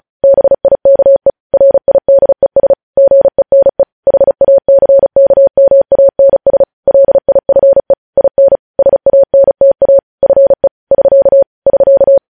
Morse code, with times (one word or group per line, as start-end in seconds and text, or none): 0.2-1.3 BIKE
1.5-2.7 RIDES
3.0-3.8 GENE
4.1-6.6 HACKMANS
6.9-7.9 LIFE
8.2-8.6 IN
8.8-10.0 SANTA
10.2-10.7 FE
10.9-11.4 <SK>
11.7-12.2 <SK>